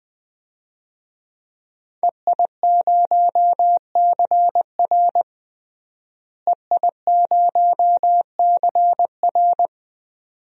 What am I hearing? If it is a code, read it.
Morse code, heard as a keyed tone, about 21 wpm, 700 Hz. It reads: EI0CR EI0CR
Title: DXCC-100 Call Signs - 5 Characters - Repeated 2x - 20wpm